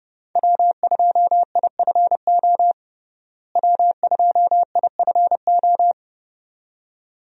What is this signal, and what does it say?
Morse code, heard as a keyed tone, about 30 wpm, 700 Hz. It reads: W2IFO W2IFO
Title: USA Call Signs - 5 Characters - Repeated 2x - 30wpm